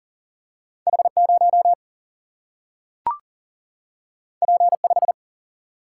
Morse code, heard as a keyed tone, about 39 wpm, 700 Hz. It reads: H0 P5
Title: Single-Letter Single-Number 40wpm